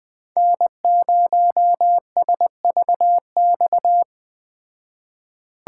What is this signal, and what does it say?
Morse code, heard as a keyed tone, about 20 wpm, 700 Hz. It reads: N0SVX
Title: Speed Racing - USA 1x3 Call Signs 17wpm